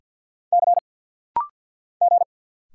Morse code, heard as a keyed tone, about 50 wpm, 690 Hz. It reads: C G
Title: Single Letters - Rapid-Fire 50wpm